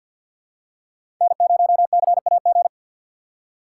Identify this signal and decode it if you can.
Morse code, heard as a keyed tone, about 53 wpm, 700 Hz. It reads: N0CAG
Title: USA Call Signs - 5 Characters - 4x Character Spacing - 50wpm